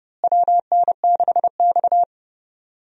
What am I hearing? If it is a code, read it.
Morse code, heard as a keyed tone, about 30 wpm, 700 Hz. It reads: WN6X
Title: USA Call Signs - 4 Characters - Repeated 3x - 30wpm